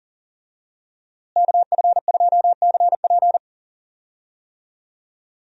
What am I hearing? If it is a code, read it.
Morse code, heard as a keyed tone, about 39 wpm, 700 Hz. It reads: KF2CP
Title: USA Call Signs - 5 Characters - Repeated 3x - 40wpm